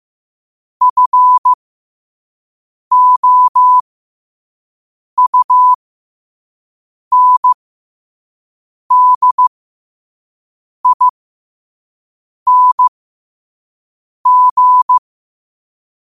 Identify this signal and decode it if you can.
Morse code, heard as a keyed tone, about 15 wpm, spaced as for 5 wpm, 1000 Hz. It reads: FOUNDING
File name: qotdfiles / 5wpm